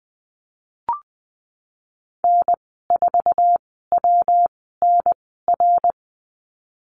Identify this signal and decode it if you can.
Morse code, heard as a keyed tone, about 20 wpm, spaced as for 15 wpm, 700 Hz. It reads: N4WNR
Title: 200 Lesson 32g - Letter k - Call Signs 20wpm